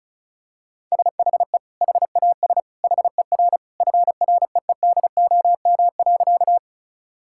Morse code, heard as a keyed tone, about 36 wpm, 700 Hz. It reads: SHE HAS HER FREEDOM.
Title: Sentences from Top 1000 Words - Review Part 5 - 35wpm